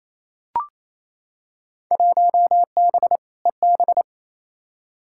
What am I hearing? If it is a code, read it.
Morse code, heard as a keyed tone, about 28 wpm, 700 Hz. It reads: 1B EB
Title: Field Day Exchanges Encore - 28wpm